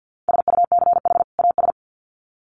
Morse code, heard as a keyed tone, about 50 wpm, 700 Hz. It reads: SUCH AS
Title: Sets of 2 Words 50wpm